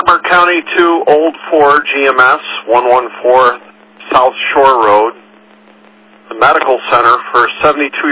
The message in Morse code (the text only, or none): none